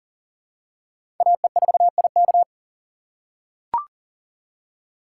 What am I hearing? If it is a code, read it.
Morse code, heard as a keyed tone, about 40 wpm, 700 Hz. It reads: AE4IK
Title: USA Call Signs - 5 Characters - 3x Character Spacing - 40wpm